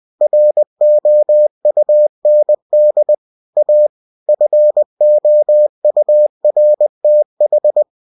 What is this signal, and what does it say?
Morse code, heard as a keyed tone, about 20 wpm, 600 Hz. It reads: ROUND A FOURTH
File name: cw-books / winnie-the-pooh / winnie-20wpm-page-0025